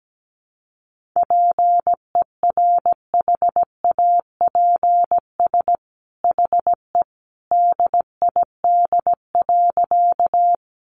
Morse code, heard as a keyed tone, about 17 wpm, 700 Hz. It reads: PERHAPS HE DID.
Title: Sentences from Top 500 Words 17wpm